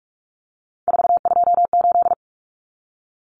Morse code, heard as a keyed tone, about 44 wpm, 700 Hz. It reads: 428
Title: Common Cut Numbers - 3-Digit 45wpm